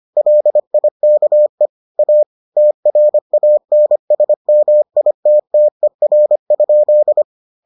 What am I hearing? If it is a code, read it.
Morse code, heard as a keyed tone, about 25 wpm, 600 Hz. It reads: LIKE A TRANSMITTER?